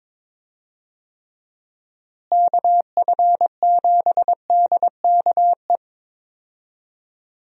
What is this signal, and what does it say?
Morse code, heard as a keyed tone, about 22 wpm, 700 Hz. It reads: KF7DKE
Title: USA Call Signs - 6 Characters - Repeated 3x - 22wpm